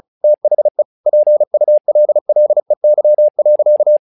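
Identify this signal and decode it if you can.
Morse code, heard as a keyed tone, about 34 wpm, 600 Hz. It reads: THE PULLEY.
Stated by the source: Morse practice text "Tower Trouble" sent at 35 WPM, from the SKCC - Straight Key Century Club learning center audio files.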